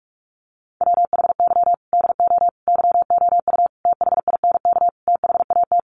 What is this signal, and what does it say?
Morse code, heard as a keyed tone, about 44 wpm, 700 Hz. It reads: WHY DO YOU THINK THAT
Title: Sets of 5 Words Encore2 45wpm